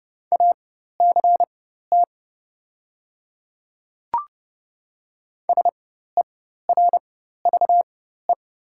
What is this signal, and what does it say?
Morse code, heard as a keyed tone, about 30 wpm, spaced as for 14 wpm, 700 Hz. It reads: ACT SERVE
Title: 4x Character Spacing - Top 500 Words - 30wpm